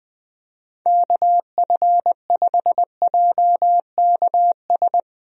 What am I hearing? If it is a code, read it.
Morse code, heard as a keyed tone, about 20 wpm, 700 Hz. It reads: KF5JKS
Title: USA Call Signs - 6 Characters - 20wpm